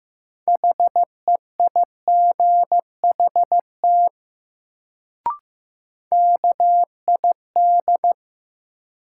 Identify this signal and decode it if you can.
Morse code, heard as a keyed tone, about 15 wpm, 700 Hz. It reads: HEIGHT KID
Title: Sentences from Top 1600 Words - New Word Drill Alt - 15wpm